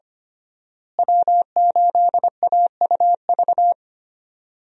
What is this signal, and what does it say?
Morse code, heard as a keyed tone, about 25 wpm, 700 Hz. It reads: W8AUV
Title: USA Call Signs - 5 Characters - Repeated 2x - 25wpm